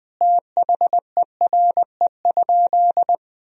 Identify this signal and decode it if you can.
Morse code, heard as a keyed tone, about 20 wpm, 700 Hz. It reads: THERE?
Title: Sentences from Top 100 Words and Abbreviations 20wpm